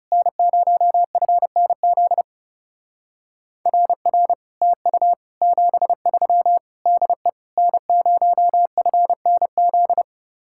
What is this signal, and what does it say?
Morse code, heard as a keyed tone, about 35 wpm, 700 Hz. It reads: N0FNZ RR TU 73 DE N0FNZ
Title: Speed-Racing Common QSO Phrases - 30wpm